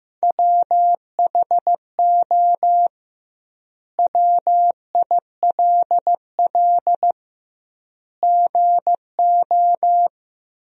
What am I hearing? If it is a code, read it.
Morse code, heard as a keyed tone, about 15 wpm, 700 Hz. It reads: WHO WILL GO
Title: 2x Word-Spacing - Sentences from Top 100 Words - 15wpm